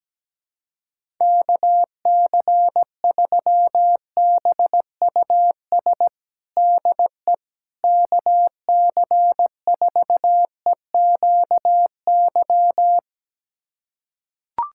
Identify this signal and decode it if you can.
Morse code, heard as a keyed tone, about 17 wpm, 700 Hz. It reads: KC3BUS DE KC4EQY
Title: USA Callsign DE Callsign 17wpm